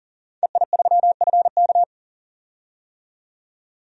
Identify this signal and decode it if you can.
Morse code, heard as a keyed tone, about 40 wpm, 700 Hz. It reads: EI3FK
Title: DXCC-100 Call Signs - 5 Characters - 40wpm